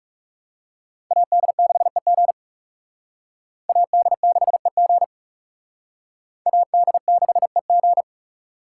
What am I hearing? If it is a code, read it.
Morse code, heard as a keyed tone, about 37 wpm, 700 Hz. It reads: AD6EG AD6EG AD6EG
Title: Speed Racing - USA 2x2 Call Signs 30wpm